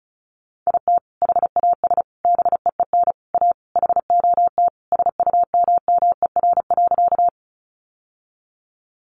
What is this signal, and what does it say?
Morse code, heard as a keyed tone, about 34 wpm, 700 Hz. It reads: IT HAS BEEN A HOT SUMMER.
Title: Sentences from Top 1100 Words 35wpm